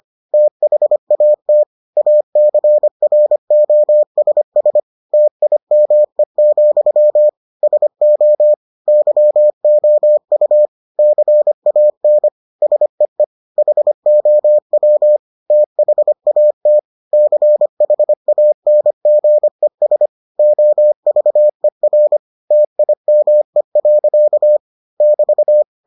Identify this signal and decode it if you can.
Morse code, heard as a keyed tone, about 25 wpm, 600 Hz. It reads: THAT ACROSS TIME, SO YOU CAN SEE HOW THAT CHANGES OVER TIME. <BT>